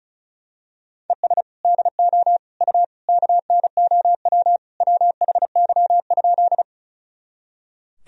Morse code, heard as a keyed tone, about 34 wpm, 700 Hz. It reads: ES DO U KNOW WHY?
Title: Sentences from Top 200 Words and Abbreviations 35wpm